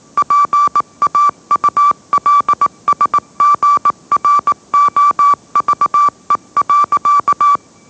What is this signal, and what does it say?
Morse code, heard as a keyed tone, about 18 wpm, 1160 Hz. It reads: PAULSGROVE.